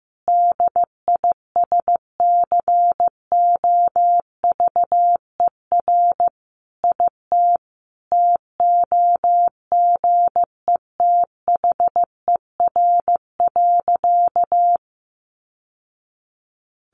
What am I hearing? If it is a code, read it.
Morse code, heard as a keyed tone, about 15 wpm, 700 Hz. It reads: DISCOVER IT TOGETHER.
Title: Sentences from Top 1100 Words 15wpm